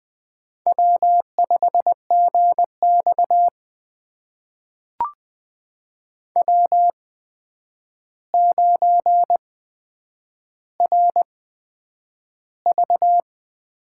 Morse code, heard as a keyed tone, about 20 wpm, spaced as for 9 wpm, 700 Hz. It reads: W5GX W9RV
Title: USA Call Signs - 8x Character Spacing - 20wpm